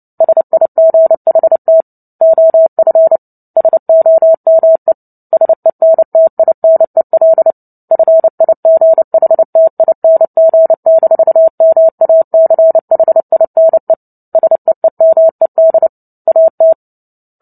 Morse code, heard as a keyed tone, about 30 wpm, 650 Hz. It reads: SIGHT OF SOME SENTINEL FIGHTING-MACHINE SEEMED AT
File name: Book Two / 03-Chapter 3 - 29 WPM